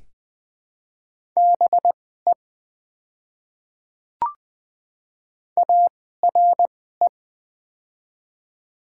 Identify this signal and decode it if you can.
Morse code, heard as a keyed tone, about 20 wpm, spaced as for 15 wpm, 700 Hz. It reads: BE ARE
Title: 2x Character Spacing - Top 100 Words - 20wpm